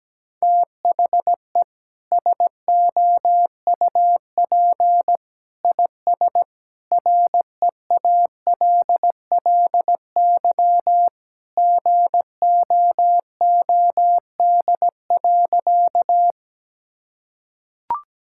Morse code, heard as a keyed tone, about 17 wpm, 700 Hz. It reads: THE SOUP IS REALLY GOOD.
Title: Sentences from Top 1000 Words - Review Part 6 - 17wpm